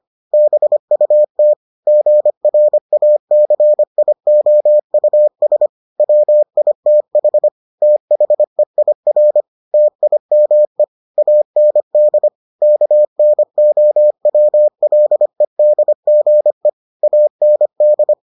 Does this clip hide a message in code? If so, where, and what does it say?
Yes: Morse code from 0.3 to 18.1 s, heard as a keyed tone, about 25 wpm, 600 Hz. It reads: BUT GRACIOUS WITH THEIR TIME AND KNOWLEDGE AND